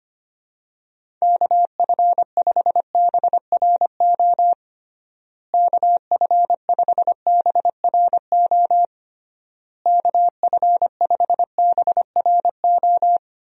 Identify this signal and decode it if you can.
Morse code, heard as a keyed tone, about 25 wpm, 700 Hz. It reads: KF5BRO KF5BRO KF5BRO
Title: USA Call Signs - 6 Characters - Repeated 3x - 25wpm